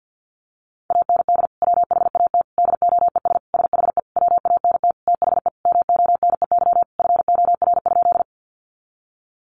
Morse code, heard as a keyed tone, about 50 wpm, 700 Hz. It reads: AND WHAT DOES SHE WANT THE MONEY FOR?
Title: Sentences from Top 400 Words 50wpm